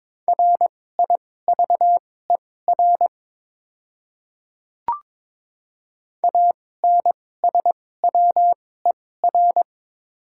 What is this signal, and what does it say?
Morse code, heard as a keyed tone, about 22 wpm, spaced as for 16 wpm, 700 Hz. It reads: RIVER ANSWER
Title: 2x Character Spacing - Top 300 Words - 22wpm